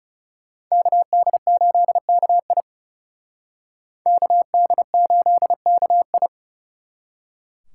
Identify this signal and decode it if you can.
Morse code, heard as a keyed tone, about 39 wpm, 700 Hz. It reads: KD8KI KD8KI